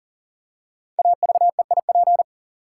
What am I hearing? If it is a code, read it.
Morse code, heard as a keyed tone, about 39 wpm, 700 Hz. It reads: AVEIP
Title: Code Groups - 5 Letters - 40wpm